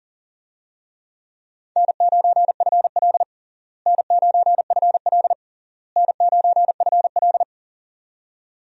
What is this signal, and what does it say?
Morse code, heard as a keyed tone, about 37 wpm, 700 Hz. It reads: N9FL N9FL N9FL
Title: USA Call Signs - 4 Characters - Repeated 3x - 40wpm